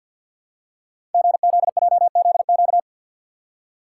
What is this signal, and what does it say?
Morse code, heard as a keyed tone, about 50 wpm, 700 Hz. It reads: GZJZQ
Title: Code Groups - 5 Letter:Numbers - 50wpm